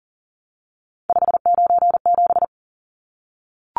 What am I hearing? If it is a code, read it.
Morse code, heard as a keyed tone, about 40 wpm, 700 Hz. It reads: 597
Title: Numbers - 3-Digit 40wpm